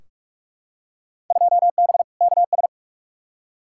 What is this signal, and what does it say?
Morse code, heard as a keyed tone, about 45 wpm, 700 Hz. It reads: 2B KS
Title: Field Day Exchanges Encore - 45wpm